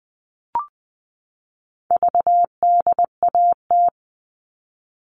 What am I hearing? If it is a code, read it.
Morse code, heard as a keyed tone, about 20 wpm, 700 Hz. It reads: VDAT
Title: All Cut Numbers - 4-Digit 20wpm